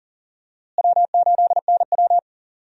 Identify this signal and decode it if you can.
Morse code, heard as a keyed tone, about 40 wpm, 700 Hz. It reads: W8NW
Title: USA Call Signs - 4 Characters - Repeated 3x - 40wpm